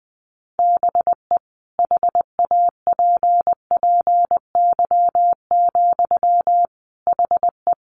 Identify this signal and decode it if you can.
Morse code, heard as a keyed tone, about 20 wpm, 700 Hz. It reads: BE HAPPY, HE
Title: Sentences from Top 1200 Words 20wpm